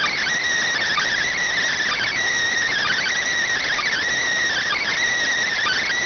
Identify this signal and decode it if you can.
Emergency Alert System (SAME) tones, heard as harsh buzzing bursts. It is heard from 1.1 to 5.0 s.